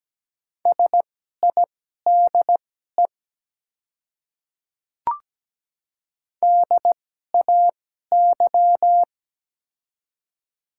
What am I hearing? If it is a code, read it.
Morse code, heard as a keyed tone, about 17 wpm, spaced as for 12 wpm, 700 Hz. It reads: SIDE DAY
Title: 2x Character Spacing - Top 100 Words - 17wpm